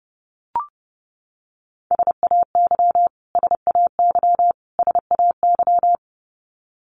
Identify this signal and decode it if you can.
Morse code, heard as a keyed tone, about 30 wpm, 700 Hz. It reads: SAY SAY SAY
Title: Top 200 Words Repeated 30wpm